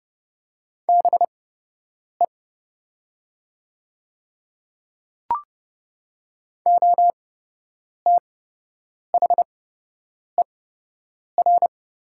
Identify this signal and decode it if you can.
Morse code, heard as a keyed tone, about 29 wpm, spaced as for 8 wpm, 700 Hz. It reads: BE OTHER
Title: 8x Character Spacing - Top 100 Words - 30wpm